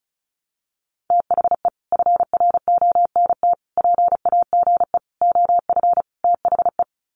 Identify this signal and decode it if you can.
Morse code, heard as a keyed tone, about 34 wpm, 700 Hz. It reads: THE FRONT PAGE OF THE
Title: Sets of 5 Words Encore 35wpm